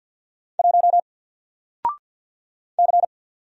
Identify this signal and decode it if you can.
Morse code, heard as a keyed tone, about 50 wpm, 700 Hz. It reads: 1 C
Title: Single Letter-Number 50wpm